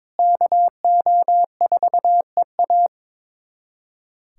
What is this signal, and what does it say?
Morse code, heard as a keyed tone, about 22 wpm, 700 Hz. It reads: KO4EA
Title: USA Callsign DE Callsign - 5 Characters - 22wpm